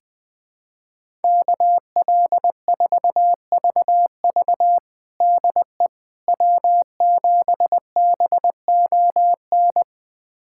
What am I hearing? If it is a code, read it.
Morse code, heard as a keyed tone, about 20 wpm, 700 Hz. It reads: KL4VV DE W7BON
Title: USA Callsign DE Callsign - 5 Characters - 20wpm